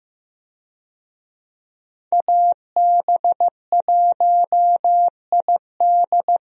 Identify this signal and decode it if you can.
Morse code, heard as a keyed tone, about 15 wpm, 700 Hz. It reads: AB1ID
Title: USA Call Signs - 5 Characters - Repeated 3x - 15wpm